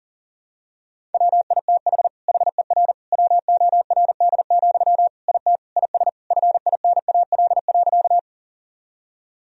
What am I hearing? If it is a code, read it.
Morse code, heard as a keyed tone, about 37 wpm, 700 Hz. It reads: WITH HER WORD, IT IS FINAL.
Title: Sentences from Top 1000 Words - Review Part 2 - 40wpm